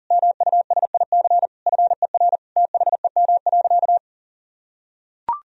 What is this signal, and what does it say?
Morse code, heard as a keyed tone, about 37 wpm, 700 Hz. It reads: MUSIC FER THEM.
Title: Sentences from Top 300 Words and Abbreviations 40wpm